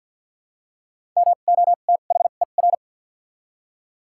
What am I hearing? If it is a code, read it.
Morse code, heard as a keyed tone, about 50 wpm, spaced as for 36 wpm, 700 Hz. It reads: MOTHER
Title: 2x Character Spacing - Top 300 Words - 50wpm